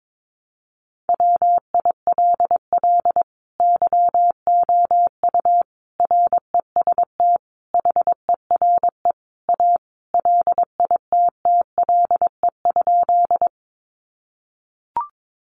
Morse code, heard as a keyed tone, about 22 wpm, 700 Hz. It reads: WILL YOU REST HERE A LITTLE?
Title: Sentences from Top 500 Words 22wpm